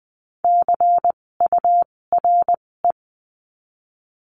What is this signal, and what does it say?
Morse code, heard as a keyed tone, about 20 wpm, spaced as for 16 wpm, 700 Hz. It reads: CURE
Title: 256 Lesson 42a - Words 12-20wpm